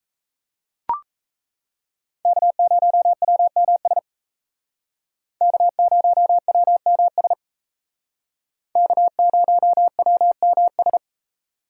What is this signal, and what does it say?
Morse code, heard as a keyed tone, about 35 wpm, 700 Hz. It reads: K0WMS K0WMS K0WMS
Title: Speed Racing - USA 1x3 Call Signs 28wpm